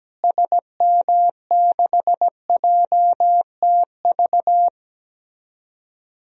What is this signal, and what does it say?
Morse code, heard as a keyed tone, about 17 wpm, 700 Hz. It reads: SM6JTV
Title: DXCC-100 Call Signs - 6 Characters - Repeated 2x - 17wpm